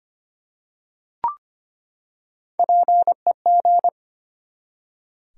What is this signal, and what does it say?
Morse code, heard as a keyed tone, about 25 wpm, 700 Hz. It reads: PEG
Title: USA Common Names - Max 3 Letters - 25wpm